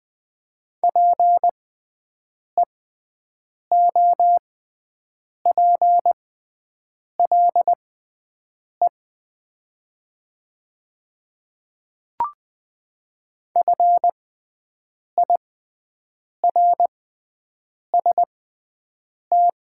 Morse code, heard as a keyed tone, about 20 wpm, spaced as for 7 wpm, 700 Hz. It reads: PEOPLE FIRST
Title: 6x Character Spacing - Top 200 Words - 20wpm